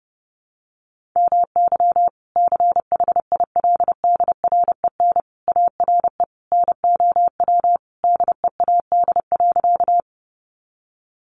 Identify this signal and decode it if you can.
Morse code, heard as a keyed tone, about 30 wpm, 700 Hz. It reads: MY CHILDREN ARE NOW DEAD.
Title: Sentences from Top 1500 Words 30wpm